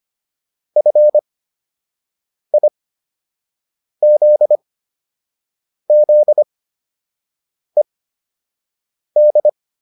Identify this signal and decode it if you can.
Morse code, heard as a keyed tone, about 25 wpm, spaced as for 6 wpm, 600 Hz. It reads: FIZZED